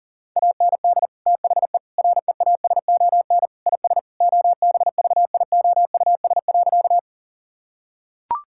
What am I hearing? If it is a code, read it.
Morse code, heard as a keyed tone, about 40 wpm, 700 Hz. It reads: AND THE REASON IS OBVIOUS.
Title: Sentences from Top 1600 Words 40wpm